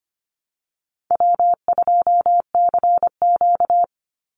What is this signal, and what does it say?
Morse code, heard as a keyed tone, about 25 wpm, 700 Hz. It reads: W2CQ
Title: Easy Call Signs Repeated 2x - 25wpm